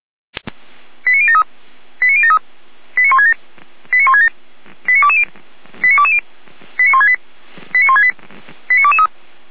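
Five-tone selective call (ZVEI), heard as quick runs of five tones, tones of about 70 ms. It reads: 89073 89073 89167 89167 89220 89220 89168 89168 89223
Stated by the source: ZVEI five-tone selective call 89011 received by SDR, recorded by the Feuerwehr Orlamünde fire brigade.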